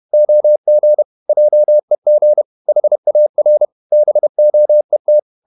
Morse code, heard as a keyed tone, about 31 wpm, 600 Hz. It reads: OG JEG HAR BOET
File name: Ferie QRQ 31wpm